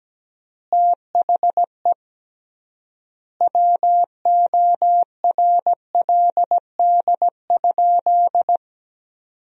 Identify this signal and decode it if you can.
Morse code, heard as a keyed tone, about 17 wpm, 700 Hz. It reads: THE WORLD?
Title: 3x Word-Spacing - Sentences from Top 200 Words - 17wpm